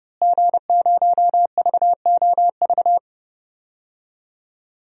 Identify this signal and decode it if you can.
Morse code, heard as a keyed tone, about 30 wpm, 700 Hz. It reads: G0VOV
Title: DXCC-100 Call Signs - 5 Characters - Repeated 2x - 30wpm